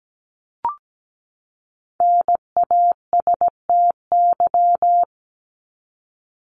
Morse code, heard as a keyed tone, about 17 wpm, 700 Hz. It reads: NASTY
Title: Top 1751-2000 Words 17wpm